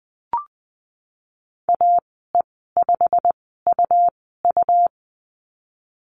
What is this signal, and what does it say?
Morse code, heard as a keyed tone, about 20 wpm, spaced as for 15 wpm, 700 Hz. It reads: AE5UU
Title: 186 Lesson 30g - Letter b - Call Signs 20wpm